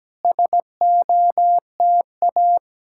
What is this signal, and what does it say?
Morse code, heard as a keyed tone, about 17 wpm, 700 Hz. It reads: SOTA
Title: SOTA QSO Practice 17wpm